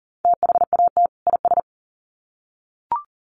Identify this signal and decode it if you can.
Morse code, heard as a keyed tone, about 40 wpm, 700 Hz. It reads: THAT IS
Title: Sets of 2 Words 40wpm